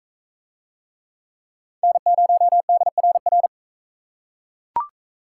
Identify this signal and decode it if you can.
Morse code, heard as a keyed tone, about 42 wpm, 700 Hz. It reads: N0DRR